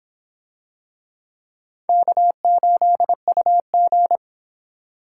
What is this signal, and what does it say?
Morse code, heard as a keyed tone, about 24 wpm, 700 Hz. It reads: K8UG